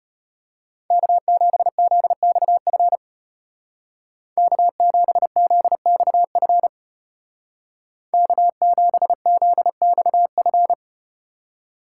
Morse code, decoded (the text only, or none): K7ZXF K7ZXF K7ZXF